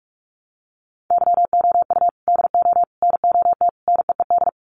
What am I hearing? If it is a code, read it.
Morse code, heard as a keyed tone, about 44 wpm, 700 Hz. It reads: YOU DO NOT NEED
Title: Sets of 4 Words 45wpm